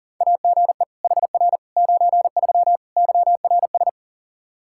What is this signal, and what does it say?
Morse code, heard as a keyed tone, about 40 wpm, 700 Hz. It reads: AGE HR 93 YRS
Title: Practice QSO - Exchange 3 - More Info - 40wpm